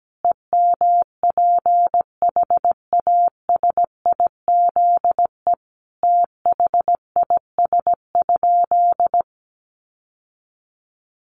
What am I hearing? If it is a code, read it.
Morse code, heard as a keyed tone, about 17 wpm, 700 Hz. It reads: EMPHASIZE THIS?
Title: Sentences from Top 1500 Words 17wpm